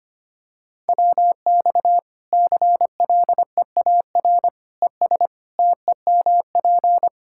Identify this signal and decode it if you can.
Morse code, heard as a keyed tone, about 25 wpm, 700 Hz. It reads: WX CLEAR ES TEMP
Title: Practice QSO - Exchange 2 - Info - 25wpm